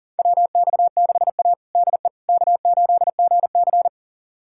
Morse code, heard as a keyed tone, about 40 wpm, 700 Hz. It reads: WX6A DE K8GC
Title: USA Callsign DE Callsign - 4 Characters - 40wpm